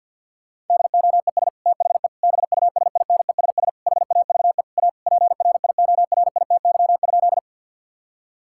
Morse code, heard as a keyed tone, about 52 wpm, 700 Hz. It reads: DOES THE BUSINESS HAVE A PRIORITY?
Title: Sentences from Top 1600 Words 50wpm